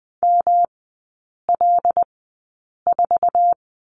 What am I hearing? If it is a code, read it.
Morse code, heard as a keyed tone, about 20 wpm, spaced as for 8 wpm, 700 Hz. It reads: ML4
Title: 212 Lesson 34e - Number 8 - Three Characters 20wpm